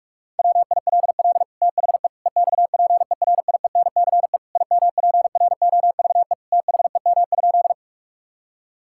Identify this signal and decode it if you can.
Morse code, heard as a keyed tone, about 41 wpm, 700 Hz. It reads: WILL THE EXPERIENCE IMPROVE THEM?
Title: Sentences from Top 1000 Words - Review Part 5 - 45wpm